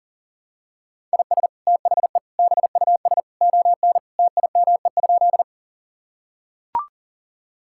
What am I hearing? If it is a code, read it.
Morse code, heard as a keyed tone, about 40 wpm, 700 Hz. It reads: IS THE BUS ON TIME?
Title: Sentences from Top 1000 Words - Review Part 6 - 40wpm